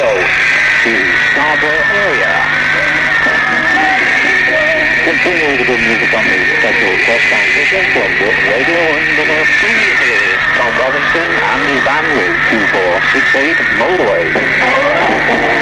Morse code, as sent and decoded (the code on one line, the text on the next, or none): none